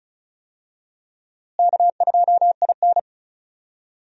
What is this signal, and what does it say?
Morse code, heard as a keyed tone, about 36 wpm, 700 Hz. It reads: K2IN
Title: USA Call Signs - 4 Characters - Repeated 2x - 35wpm